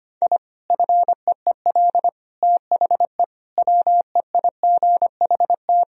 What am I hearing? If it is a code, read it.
Morse code, heard as a keyed tone, about 25 wpm, 700 Hz. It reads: I FEEL THE WEIGHT
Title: Sentences from Top 1000 Words - Review Part 3 - 25wpm